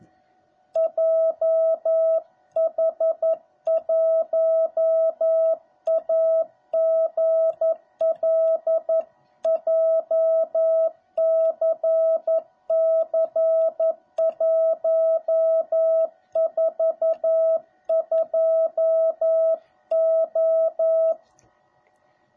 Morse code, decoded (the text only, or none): JH1AGL JCC142O